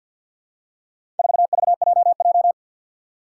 Morse code, heard as a keyed tone, about 50 wpm, 700 Hz. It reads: 4VJJ
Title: Four Challenging Letter-Number Combinations 50wpm